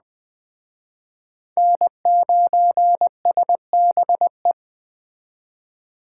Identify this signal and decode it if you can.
Morse code, heard as a keyed tone, about 20 wpm, 700 Hz. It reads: N9SBE